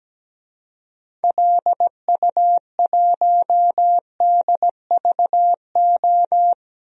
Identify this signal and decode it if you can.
Morse code, heard as a keyed tone, about 17 wpm, 700 Hz. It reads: LU1DVO
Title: DXCC-100 Call Signs - 6 Characters - 17wpm